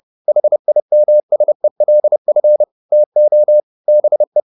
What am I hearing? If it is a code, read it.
Morse code, heard as a keyed tone, about 30 wpm, 600 Hz. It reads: HIMSELF TO BE